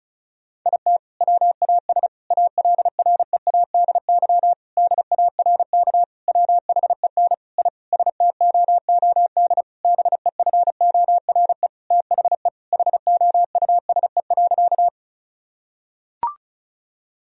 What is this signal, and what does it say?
Morse code, heard as a keyed tone, about 36 wpm, 700 Hz. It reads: IT WAS ALREADY DARK WHEN I STOOD BEFORE THE HOUSE.
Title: Sentences from Top 1000 Words - Review Part 2 - 35wpm